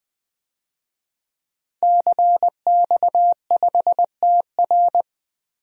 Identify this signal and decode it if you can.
Morse code, heard as a keyed tone, about 21 wpm, 700 Hz. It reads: CX5TR
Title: DXCC-100 Call Signs - 5 Characters - Repeated 3x - 20wpm